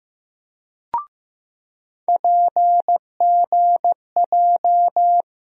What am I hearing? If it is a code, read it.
Morse code, heard as a keyed tone, about 15 wpm, 700 Hz. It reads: PGJ
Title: USA Call Sign Postfixes - 3 Letters 15wpm